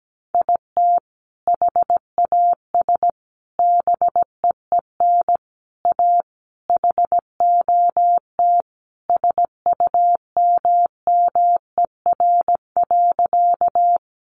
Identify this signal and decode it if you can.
Morse code, heard as a keyed tone, about 17 wpm, 700 Hz. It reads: IT HAS BEEN A HOT SUMMER.
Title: Sentences from Top 1100 Words 17wpm